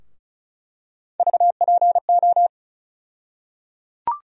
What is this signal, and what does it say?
Morse code, heard as a keyed tone, about 34 wpm, 700 Hz. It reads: VPO
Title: USA Call Sign Postfixes - 3 Letters 35wpm